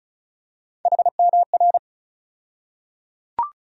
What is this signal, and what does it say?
Morse code, heard as a keyed tone, about 36 wpm, 700 Hz. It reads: HMR
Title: USA Call Sign Postfixes - 3 Letters 35wpm